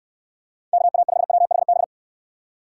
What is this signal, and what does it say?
Morse code, heard as a keyed tone, about 66 wpm, 700 Hz. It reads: KA5WHB